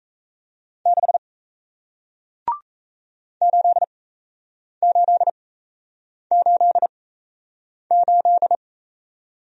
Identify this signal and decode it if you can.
Morse code, heard as a keyed tone, about 38 wpm, spaced as for 8 wpm, 700 Hz. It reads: 6 8888